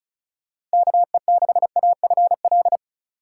Morse code, heard as a keyed tone, about 36 wpm, 700 Hz. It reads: KE6AFL
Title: USA Call Signs - 6 Characters - 3x Character Spacing - 35wpm